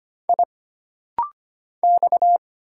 Morse code, heard as a keyed tone, about 25 wpm, 700 Hz. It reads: I X
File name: Single Character (Letter:Number:Punctuation:Prosign) - Rapid-Fire - 25wpm